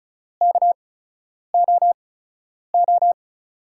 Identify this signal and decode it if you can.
Morse code, heard as a keyed tone, about 36 wpm, spaced as for 10 wpm, 700 Hz. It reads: KOO
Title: USA Call Signs - 5 Characters - 8x Character Spacing - 35wpm